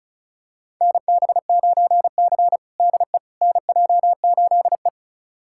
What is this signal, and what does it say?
Morse code, heard as a keyed tone, about 34 wpm, 690 Hz. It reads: NB9C DE NJ8E
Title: USA Callsign DE Callsign - 4 Characters - 35wpm